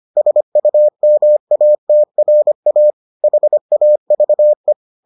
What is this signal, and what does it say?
Morse code, heard as a keyed tone, about 25 wpm, 600 Hz. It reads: SUMATRA HAVE